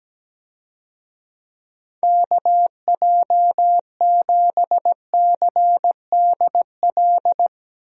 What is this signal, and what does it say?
Morse code, heard as a keyed tone, about 17 wpm, 700 Hz. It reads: KJ7CDL
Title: USA Call Signs - 6 Characters - Repeated 3x - 17wpm